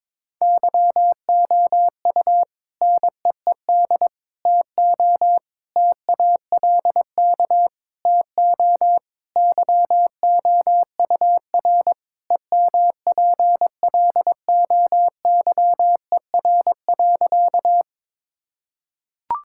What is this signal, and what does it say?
Morse code, heard as a keyed tone, about 22 wpm, 700 Hz. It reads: YOU NEED TO TALK TO YOUR EMPLOYER.
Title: Sentences from Top 1600 Words 22wpm